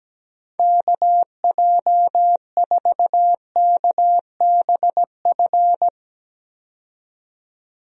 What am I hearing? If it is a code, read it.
Morse code, heard as a keyed tone, about 17 wpm, 700 Hz. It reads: KJ4KBF
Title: USA Call Signs - 6 Characters - 17wpm